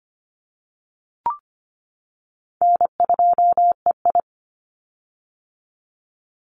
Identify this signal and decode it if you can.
Morse code, heard as a keyed tone, about 25 wpm, 700 Hz. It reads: N2EI
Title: Easy Call Signs - 25wpm